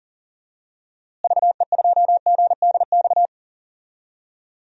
Morse code, heard as a keyed tone, about 40 wpm, 700 Hz. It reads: VE2GDX
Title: DXCC-100 Call Signs - 6 Characters - Repeated 3x - 40wpm